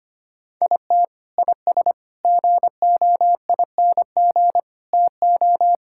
Morse code, heard as a keyed tone, about 25 wpm, 700 Hz. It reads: IT IS GOING TO
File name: practice / 25wpm / Sentences with 3-Word Phrases - 25wpm